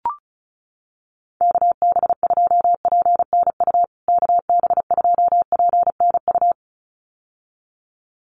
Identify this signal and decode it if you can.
Morse code, heard as a keyed tone, about 34 wpm, 700 Hz. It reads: KB2PNU KB2PNU
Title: USA Call Signs Repeated 2x 35wpm